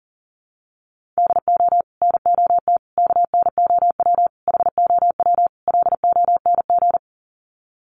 Morse code, heard as a keyed tone, about 40 wpm, 700 Hz. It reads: DO NOT KNOW HOW LONG
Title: Sets of 5 Words Encore2 40wpm